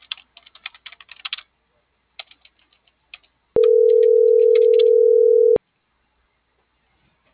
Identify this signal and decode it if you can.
Ringback tone (North America), 440 + 480 Hz.